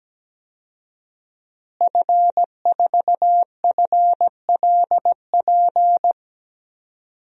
Morse code, heard as a keyed tone, about 17 wpm, 700 Hz. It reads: F4FLP
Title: DXCC-100 Call Signs - 5 Characters - Repeated 3x - 17wpm